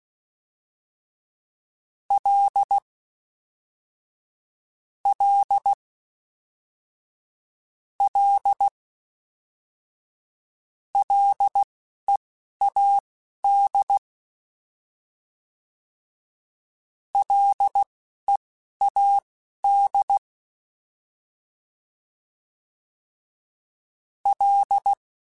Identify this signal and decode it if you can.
Morse code, heard as a keyed tone, about 16 wpm, spaced as for 11 wpm, 800 Hz. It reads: L L L LEAD LEAD L
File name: Ls / Single L m3s / Single L Prefixe 3x2 Twice @10